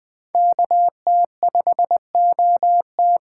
Morse code, heard as a keyed tone, about 20 wpm, 700 Hz. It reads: KT5OT